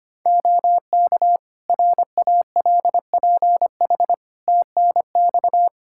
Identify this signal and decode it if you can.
Morse code, heard as a keyed tone, about 25 wpm, 700 Hz. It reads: OK RALPH TNX
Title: Practice QSO - Exchange 4 - Ending - 25wpm